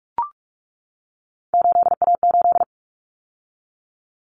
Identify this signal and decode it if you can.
Morse code, heard as a keyed tone, about 44 wpm, 700 Hz. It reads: 8A8
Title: Common Cut Numbers - 3-Digit 45wpm